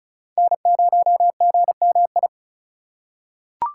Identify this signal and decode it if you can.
Morse code, heard as a keyed tone, about 34 wpm, 700 Hz. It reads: N0GMI
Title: USA Call Signs - 5 Characters - 4x Character Spacing - 35wpm